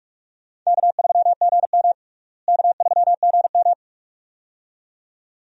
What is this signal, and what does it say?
Morse code, heard as a keyed tone, about 46 wpm, 700 Hz. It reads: K3GM K3GM
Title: USA Call Signs - 4 Characters - Repeated 2x - 45wpm